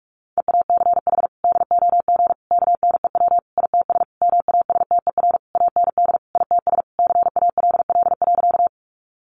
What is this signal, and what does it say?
Morse code, heard as a keyed tone, about 44 wpm, 700 Hz. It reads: EACH DOG KNEW ITS MASTER AND ITS CALL.
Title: Sentences from Top 1300 Words 45wpm